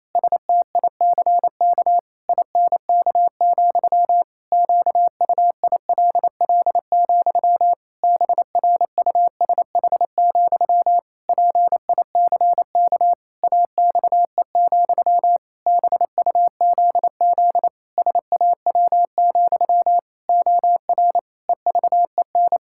Morse code, heard as a keyed tone, about 28 wpm, 700 Hz. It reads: STICK INK, QUILL, BRUSH, PICK AXE, BUZZ SAW, OR EVEN
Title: Pangram Challenge v2 - 28wpm